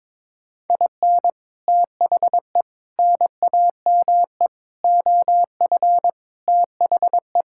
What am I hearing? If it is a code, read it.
Morse code, heard as a keyed tone, about 21 wpm, 700 Hz. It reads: IN THE NAME OF THE
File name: Sentences with 9-Word Phrases - 22wpm